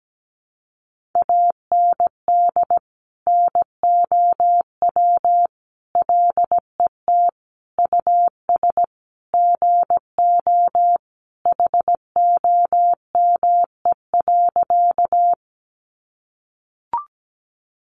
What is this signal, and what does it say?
Morse code, heard as a keyed tone, about 17 wpm, 700 Hz. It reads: AND NOW LET US GO HOME.
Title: Sentences from Top 300 Words 17wpm